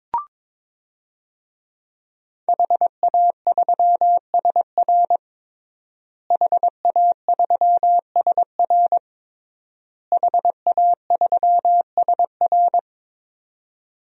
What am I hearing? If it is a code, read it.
Morse code, heard as a keyed tone, about 22 wpm, 700 Hz. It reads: HA3SR HA3SR HA3SR
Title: DXCC-100 Call Signs - 5 Characters - Repeated 3x - 22wpm